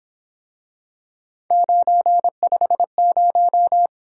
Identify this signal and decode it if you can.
Morse code, heard as a keyed tone, about 26 wpm, 700 Hz. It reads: 950
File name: Speed-Racing - Three Letter-Number Combinations - 22wpm